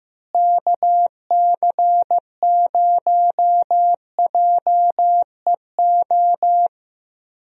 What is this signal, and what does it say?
Morse code, heard as a keyed tone, about 14 wpm, 700 Hz. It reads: KC0JEO
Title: USA Call Signs - 6 Characters - Repeated 3x - 15wpm